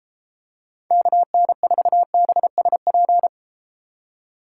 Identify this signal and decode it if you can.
Morse code, heard as a keyed tone, about 32 wpm, 700 Hz. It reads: KN4BSP